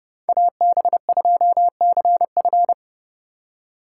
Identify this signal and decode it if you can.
Morse code, heard as a keyed tone, about 29 wpm, 700 Hz. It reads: AB2CF
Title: USA Call Signs - 5 Characters - 4x Character Spacing - 30wpm